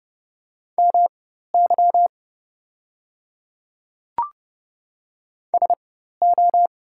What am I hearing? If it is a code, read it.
Morse code, heard as a keyed tone, about 31 wpm, spaced as for 14 wpm, 690 Hz. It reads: MY SO
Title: 4x Character Spacing - Top 100 Words - 30wpm